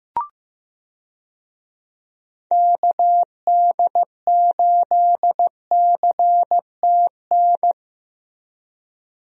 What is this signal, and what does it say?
Morse code, heard as a keyed tone, about 15 wpm, 700 Hz. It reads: KD8CTN
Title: USA Call Signs - 6 Characters - 15wpm